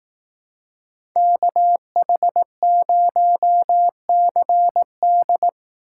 Morse code, heard as a keyed tone, about 18 wpm, 700 Hz. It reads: KH0CD
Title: Speed Racing - USA 2x2 Call Signs 15wpm